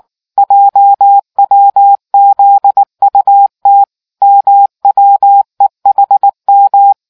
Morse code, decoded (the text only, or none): JWZUT MWEHM